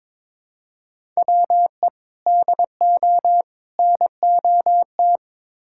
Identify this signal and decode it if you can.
Morse code, heard as a keyed tone, about 22 wpm, 700 Hz. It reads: WE DO NOT
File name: practice / 22wpm / Build A Phrase - 5-Word Phrases - 22wpm